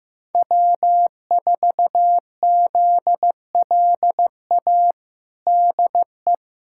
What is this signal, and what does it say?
Morse code, heard as a keyed tone, about 15 wpm, 700 Hz. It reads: W4ZLA DE
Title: USA Callsign DE Callsign 15wpm